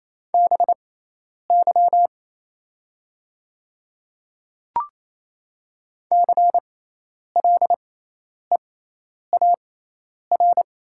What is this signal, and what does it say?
Morse code, heard as a keyed tone, about 28 wpm, spaced as for 10 wpm, 690 Hz. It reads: BY CLEAR
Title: 6x Character Spacing - Top 500 Words - 28wpm